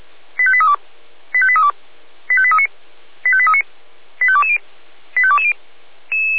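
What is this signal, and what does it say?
Five-tone selective call (ZVEI), heard as quick runs of five tones, tones of about 70 ms. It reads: 85832 85832 85839 85839 85220 85220
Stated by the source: ZVEI five-tone selective call received on SDR.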